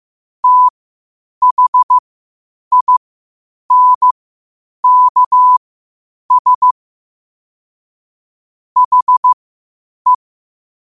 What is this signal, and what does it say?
Morse code, heard as a keyed tone, about 15 wpm, spaced as for 8 wpm, 1000 Hz. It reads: THINKS HE
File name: qotdfiles / 8wpm